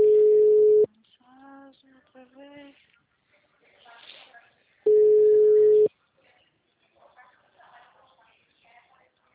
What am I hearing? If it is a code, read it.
Ringback tone (Europe, 425 Hz), with speech.